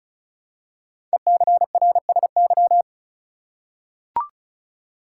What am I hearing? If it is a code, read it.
Morse code, heard as a keyed tone, about 36 wpm, 700 Hz. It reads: ECRSY
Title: Code Groups - 5 Letters - 35wpm